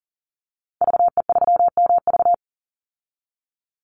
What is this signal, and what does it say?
Morse code, heard as a keyed tone, about 40 wpm, 700 Hz. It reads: VE3MV
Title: Call Signs Repeated 3x - 40wpm